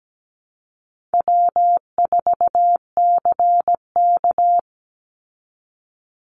Morse code, heard as a keyed tone, about 17 wpm, 700 Hz. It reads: W4CK
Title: Easy Call Signs Repeated 2x - 17wpm